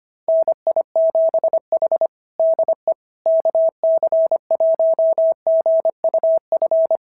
Morse code, heard as a keyed tone, about 22 wpm, 650 Hz. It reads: NI7H DE KC1GUF
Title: SOTA QSO Practice 25wpm